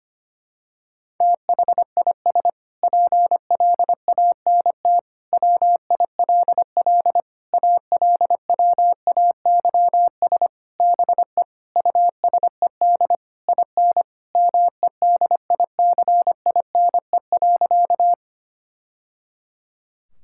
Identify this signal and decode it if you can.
Morse code, heard as a keyed tone, about 25 wpm, 700 Hz. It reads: THIS PLANT WILL ALWAYS BE USED IN MEDICINE.
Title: Sentences from Top 1000 Words - Review Part 6 - 25wpm